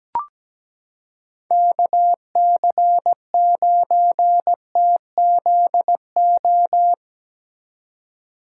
Morse code, heard as a keyed tone, about 17 wpm, 700 Hz. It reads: KC9TZO
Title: K1USN SST QSO Practice - 17wpm